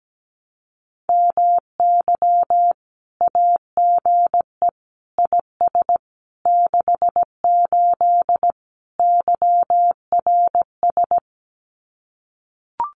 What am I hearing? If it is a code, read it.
Morse code, heard as a keyed tone, about 17 wpm, 700 Hz. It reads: MY AGE IS 68 YRS
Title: Common QSO Phrases Encore 17wpm